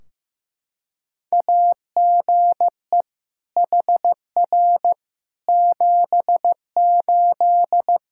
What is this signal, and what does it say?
Morse code, heard as a keyed tone, about 15 wpm, 700 Hz. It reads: AGE HR 78
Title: Practice QSO - Exchange 3 - More Info - 15wpm